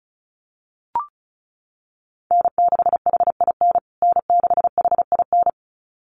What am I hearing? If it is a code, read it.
Morse code, heard as a keyed tone, about 34 wpm, 700 Hz. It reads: N6HIN N6HIN
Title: USA Call Signs Repeated 2x 35wpm